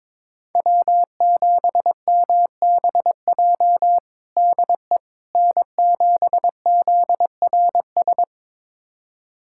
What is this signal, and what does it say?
Morse code, heard as a keyed tone, about 22 wpm, 700 Hz. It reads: W7MBJ DE N7ZRS
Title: USA Callsign DE Callsign - 5 Characters - 22wpm